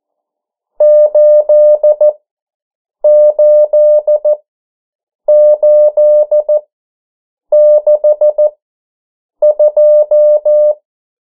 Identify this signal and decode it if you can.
Morse code, heard as a keyed tone, about 14 wpm, spaced as for 7 wpm, 600 Hz. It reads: 88862
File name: morse mp3 / 09 urok 8